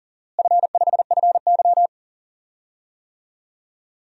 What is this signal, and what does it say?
Morse code, heard as a keyed tone, about 40 wpm, 700 Hz. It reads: F5FY
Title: DXCC-100 Call Signs - 4 Characters - Repeated 2x - 40wpm